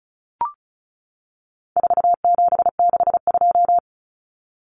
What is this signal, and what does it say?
Morse code, heard as a keyed tone, about 34 wpm, 700 Hz. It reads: 4762
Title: Numbers - 4-Digit 35wpm